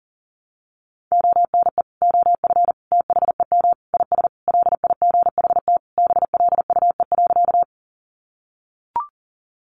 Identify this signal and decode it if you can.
Morse code, heard as a keyed tone, about 40 wpm, 700 Hz. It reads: ONE OF THEM IS LIGHT BLUE.
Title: Additional Easy Phrases 40wpm